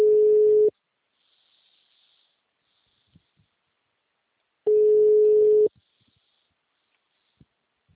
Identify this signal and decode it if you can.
Ringback tone (Europe, 425 Hz).